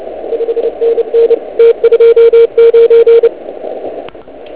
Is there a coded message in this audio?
Yes: Morse code from 0.3 to 3.3 s, heard as a keyed tone, about 29 wpm, 460 Hz, with radio noise. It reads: 5NN T29